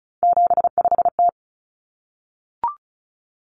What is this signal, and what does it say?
Morse code, heard as a keyed tone, about 34 wpm, 700 Hz. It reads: 75T
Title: Common Cut Numbers - 3-Digit 35wpm